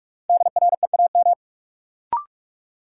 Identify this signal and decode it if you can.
Morse code, heard as a keyed tone, about 46 wpm, 700 Hz. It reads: DREAM
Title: Sentences from Top 1500 Words - New Word Drill - 45wpm